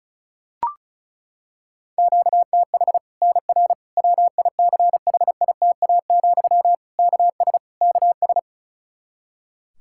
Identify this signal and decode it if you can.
Morse code, heard as a keyed tone, about 34 wpm, 700 Hz. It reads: QTH NR WICHITA, KS KS
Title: Practice QSO - Exchange 1 - Beginning - 35wpm